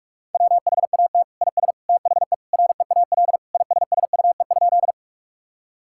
Morse code, heard as a keyed tone, about 46 wpm, 700 Hz. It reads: WHAT IS THE REAL ISSUE?
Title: Sentences from Top 1000 Words - Review Part 6 - 45wpm